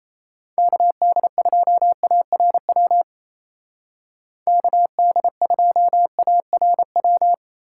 Morse code, heard as a keyed tone, about 32 wpm, 700 Hz. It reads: KD2ARW KD2ARW
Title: Speed Racing - USA 2x3 Call Signs 28wpm